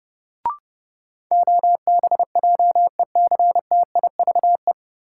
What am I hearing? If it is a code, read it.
Morse code, heard as a keyed tone, about 30 wpm, 700 Hz. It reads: OBJECTIVE
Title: Sentences from Top 1200 Words - New Word Drill - 30wpm